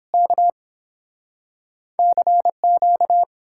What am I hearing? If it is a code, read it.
Morse code, heard as a keyed tone, about 31 wpm, 700 Hz. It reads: K CQ